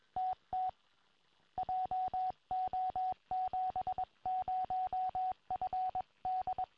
Morse code, heard as a keyed tone, about 22 wpm, 740 Hz, with radio noise. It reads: TT JO70FD
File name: tropo / yl3iq